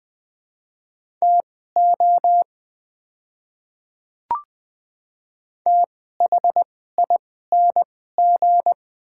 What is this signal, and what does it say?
Morse code, heard as a keyed tone, about 20 wpm, spaced as for 14 wpm, 700 Hz. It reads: TO THING
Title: 2x Character Spacing - Top 100 Words - 20wpm